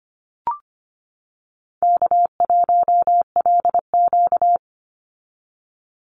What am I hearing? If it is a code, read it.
Morse code, heard as a keyed tone, about 25 wpm, 700 Hz. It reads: K1LQ
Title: USA Call Signs 25wpm